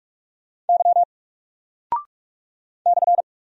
Morse code, heard as a keyed tone, about 45 wpm, 700 Hz. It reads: Y /
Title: Single Character (Letter:Number:Punctuation:Prosign) Alt 45wpm